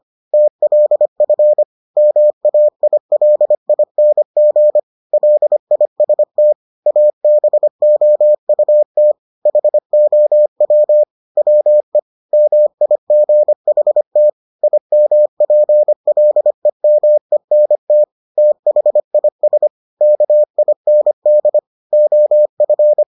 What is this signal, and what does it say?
Morse code, heard as a keyed tone, about 25 wpm, 600 Hz. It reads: TLF MAILING LIST ABOUT HOW WE MIGHT IMPLEMENT THIS KIND OF